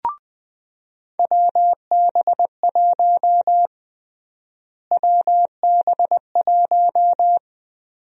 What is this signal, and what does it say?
Morse code, heard as a keyed tone, about 21 wpm, 700 Hz. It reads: WB1 WB1
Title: USA Call Sign Prefixes - Repeated 2x - 20wpm